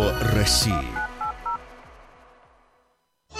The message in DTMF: 1980